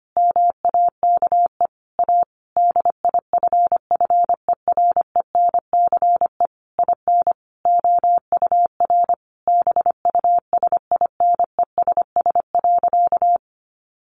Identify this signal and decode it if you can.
Morse code, heard as a keyed tone, about 25 wpm, 700 Hz. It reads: MAKE A DIFFERENCE IN OUR BUSINESS.
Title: Sentences from Top 1400 Words 25wpm